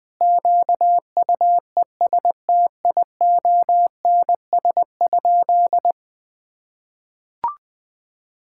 Morse code, heard as a keyed tone, about 20 wpm, 700 Hz. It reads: QUESTIONS?
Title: Sentences with 4-Word Phrases - 20wpm